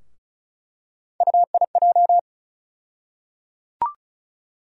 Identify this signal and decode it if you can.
Morse code, heard as a keyed tone, about 34 wpm, 700 Hz. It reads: UIJ
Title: USA Call Sign Postfixes - 3 Letters 35wpm